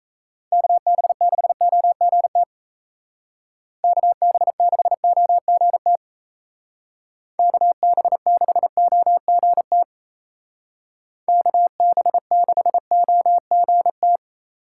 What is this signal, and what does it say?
Morse code, heard as a keyed tone, about 36 wpm, 700 Hz. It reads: KB6OGT KB6OGT KB6OGT KB6OGT